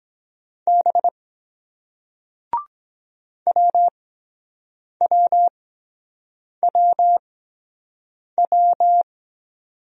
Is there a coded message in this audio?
Yes: Morse code from 0.7 to 9.0 s, heard as a keyed tone, about 23 wpm, spaced as for 7 wpm, 700 Hz. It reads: B WWWW